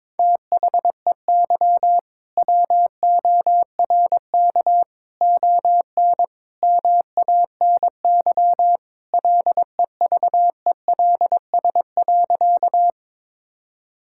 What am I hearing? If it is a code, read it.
Morse code, heard as a keyed tone, about 22 wpm, 700 Hz. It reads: THEY WORK ON MANY LEVELS.
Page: morse-code-ninja-practice-22wpm